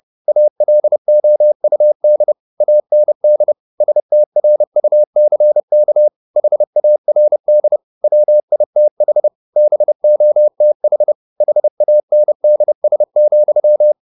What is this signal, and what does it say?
Morse code, heard as a keyed tone, about 30 wpm, 600 Hz. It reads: ALOUD AND STRUCK HARD WITH BOTH HANDS,
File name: the old man and the sea / the old man and the sea 1 30wpm